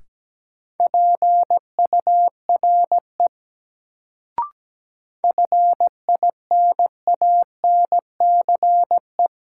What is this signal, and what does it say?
Morse code, heard as a keyed tone, about 17 wpm, 700 Hz. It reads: PURE FINANCE
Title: Sentences from Top 1500 Words - New Word Drill Alt - 17wpm